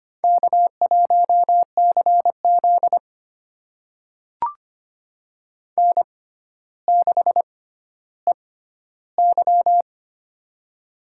Morse code, heard as a keyed tone, about 25 wpm, spaced as for 13 wpm, 700 Hz. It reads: K1CZ N6EY
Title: USA Call Signs - 4 Characters - 6x Character Spacing - 25wpm